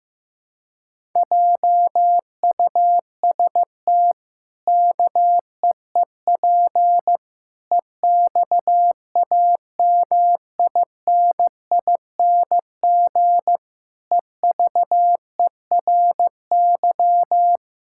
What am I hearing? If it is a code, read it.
Morse code, heard as a keyed tone, about 15 wpm, 700 Hz. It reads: JUST KEEP EXAMINING EVERY
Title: Pangram Challenge v2 - 15wpm